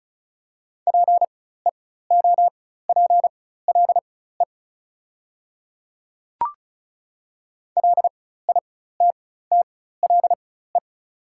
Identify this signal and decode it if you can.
Morse code, heard as a keyed tone, about 34 wpm, spaced as for 16 wpm, 690 Hz. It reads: PEOPLE LITTLE
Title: 4x Character Spacing - Top 200 Words - 35wpm